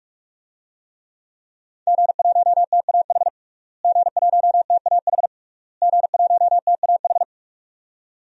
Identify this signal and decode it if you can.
Morse code, heard as a keyed tone, about 42 wpm, 700 Hz. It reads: G1TAH G1TAH G1TAH
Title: DXCC-100 Call Signs - 5 Characters - Repeated 3x - 45wpm